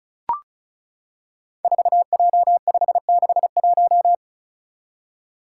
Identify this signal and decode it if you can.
Morse code, heard as a keyed tone, about 36 wpm, 700 Hz. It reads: 4J561
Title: Five Challenging Letter-Number Combinations 35wpm